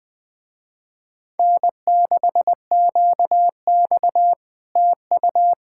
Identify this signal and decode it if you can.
Morse code, heard as a keyed tone, about 20 wpm, 700 Hz. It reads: N6QX TU
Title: Speed-Racing Common QSO Phrases - 17wpm